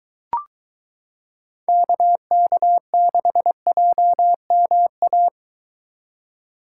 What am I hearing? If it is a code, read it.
Morse code, heard as a keyed tone, about 22 wpm, 700 Hz. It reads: KK6JMA